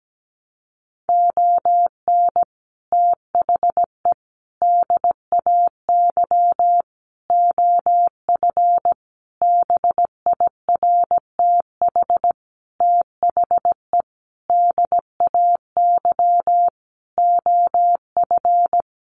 Morse code, heard as a keyed tone, about 17 wpm, 700 Hz. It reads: ON THE DAY OF BIRTH THE DAY OF
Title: Sentences from Top 1100 Words 17wpm